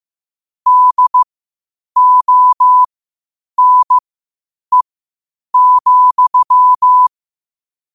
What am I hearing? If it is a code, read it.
Morse code, heard as a keyed tone, about 15 wpm, spaced as for 8 wpm, 1000 Hz. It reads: DONE,
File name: qotdfiles / 8wpm